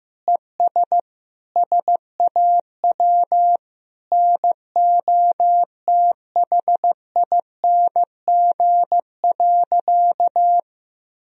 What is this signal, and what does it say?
Morse code, heard as a keyed tone, about 15 wpm, 700 Hz. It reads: ES SAW NOTHING.
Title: Sentences from Top 500 Words and Abbreviations 15wpm